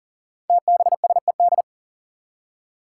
Morse code, heard as a keyed tone, about 42 wpm, 700 Hz. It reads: TBSED
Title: Five Challenging Letter-Number Combinations 40wpm